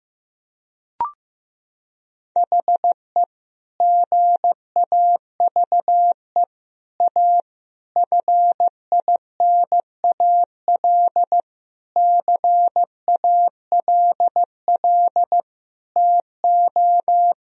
Morse code, heard as a keyed tone, about 15 wpm, 700 Hz. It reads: HE GAVE A FINAL CALL TO
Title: Sentences from Top 1000 Words - Review Part 2 - 15wpm